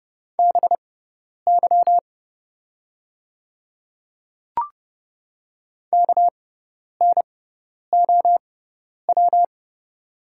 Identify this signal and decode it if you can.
Morse code, heard as a keyed tone, about 30 wpm, spaced as for 10 wpm, 700 Hz. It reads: BY KNOW
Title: 6x Character Spacing - Top 100 Words - 30wpm